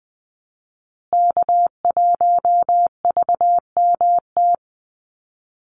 Morse code, heard as a keyed tone, about 20 wpm, 700 Hz. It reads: K1VMT
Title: Call Signs Repeated 2x - 20wpm